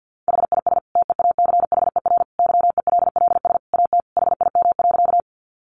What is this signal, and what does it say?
Morse code, heard as a keyed tone, about 48 wpm, 700 Hz. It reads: HIS TEACHER YELLS AT HIM.
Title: Additional Easy Phrases 50wpm